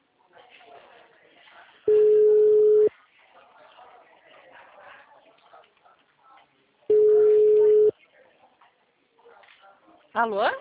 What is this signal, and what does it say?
Ringback tone (Europe, 425 Hz), with speech.